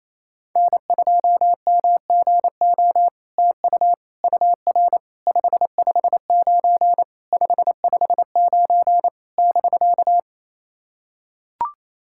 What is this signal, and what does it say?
Morse code, heard as a keyed tone, about 28 wpm, 700 Hz. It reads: N2MGO TU UR 559 559 <BK>
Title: SOTA QSO Practice 28wpm